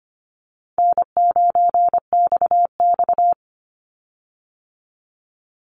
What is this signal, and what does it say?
Morse code, heard as a keyed tone, about 25 wpm, 700 Hz. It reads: N9XX
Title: Easy Call Signs Repeated 3x - 25wpm